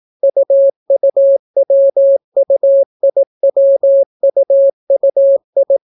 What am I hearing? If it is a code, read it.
Morse code, heard as a keyed tone, about 18 wpm, 550 Hz. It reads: UUWUIWUUI